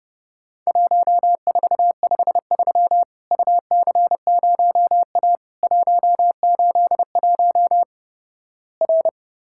Morse code, heard as a keyed tone, about 29 wpm, 700 Hz. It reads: 1453 UC0A 181 R
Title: Worked All Europe DX Contest - QTC Exchange - 30wpm